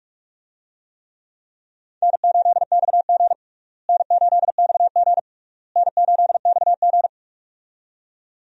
Morse code, heard as a keyed tone, about 41 wpm, 700 Hz. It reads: N8XG N8XG N8XG
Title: USA Call Signs - 4 Characters - Repeated 3x - 45wpm